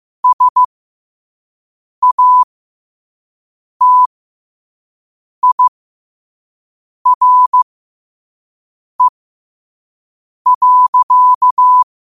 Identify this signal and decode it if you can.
Morse code, heard as a keyed tone, about 15 wpm, spaced as for 5 wpm, 1000 Hz. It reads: SATIRE.
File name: 5wpm